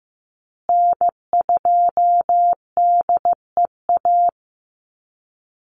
Morse code, heard as a keyed tone, about 15 wpm, 700 Hz. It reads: N2DEA
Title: USA Call Signs Repeated 3x 15wpm